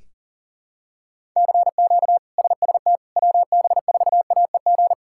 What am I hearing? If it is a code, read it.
Morse code, heard as a keyed tone, about 37 wpm, 700 Hz. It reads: CQ SST WB4AEG
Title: K1USN SST QSO Practice - 40wpm